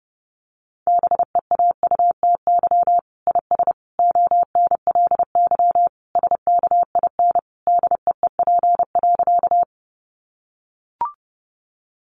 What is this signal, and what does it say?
Morse code, heard as a keyed tone, about 30 wpm, 700 Hz. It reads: BEAUTY IS ONLY SKIN DEEP.
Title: Sentences from Top 1300 Words 30wpm